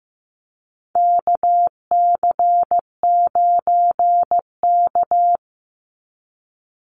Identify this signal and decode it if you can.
Morse code, heard as a keyed tone, about 15 wpm, 700 Hz. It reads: KC9K
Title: Call Signs Repeated 2x - 15wpm